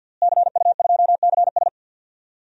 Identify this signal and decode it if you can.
Morse code, heard as a keyed tone, about 46 wpm, 700 Hz. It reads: CU2CS
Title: DXCC-100 Call Signs - 5 Characters - Repeated 3x - 50wpm